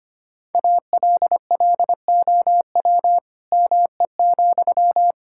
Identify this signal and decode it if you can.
Morse code, heard as a keyed tone, about 25 wpm, 700 Hz. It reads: ALLOW ME,
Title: Sentences from Top 1000 Words - Review Part 5 - 25wpm